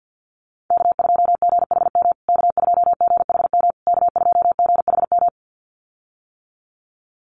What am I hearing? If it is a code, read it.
Morse code, heard as a keyed tone, about 50 wpm, 700 Hz. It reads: K2GHM K2GHM K2GHM
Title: USA Call Signs Repeated 3x 50wpm